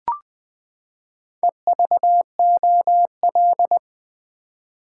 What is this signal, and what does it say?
Morse code, heard as a keyed tone, about 20 wpm, 700 Hz. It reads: EVOL
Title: Build A Word - 4 Letters At A Time - 12-Letter Words - 20wpm